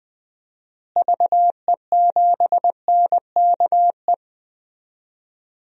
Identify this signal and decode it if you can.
Morse code, heard as a keyed tone, about 20 wpm, 700 Hz. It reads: VE7NKE
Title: DXCC-100 Call Signs - 6 Characters - Repeated 3x - 20wpm